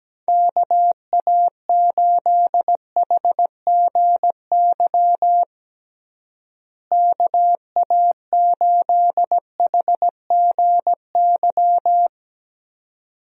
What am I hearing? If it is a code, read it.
Morse code, heard as a keyed tone, about 17 wpm, 700 Hz. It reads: KA8HGY KA8HGY
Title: USA Call Signs - 6 Characters - Repeated 2x - 17wpm